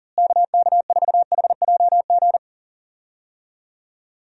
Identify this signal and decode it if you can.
Morse code, heard as a keyed tone, about 40 wpm, 700 Hz. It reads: KK4HJG
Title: USA Call Signs - 6 Characters - Repeated 3x - 40wpm